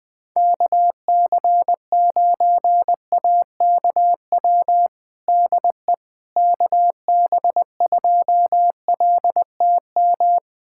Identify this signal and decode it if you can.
Morse code, heard as a keyed tone, about 21 wpm, 700 Hz. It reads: KC9AKW DE KB2LTM
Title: USA Callsign DE Callsign 20wpm